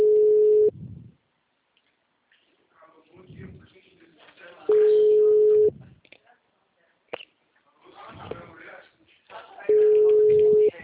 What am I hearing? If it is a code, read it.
Ringback tone (Europe, 425 Hz), with speech.
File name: out- -2007- - - .15732